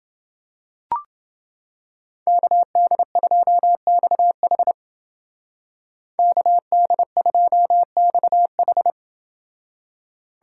Morse code, heard as a keyed tone, about 29 wpm, 700 Hz. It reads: KD2XH KD2XH